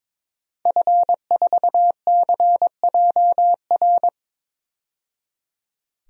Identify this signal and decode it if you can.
Morse code, heard as a keyed tone, about 22 wpm, 700 Hz. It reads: F4CJR
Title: DXCC-100 Call Signs - 5 Characters - 22wpm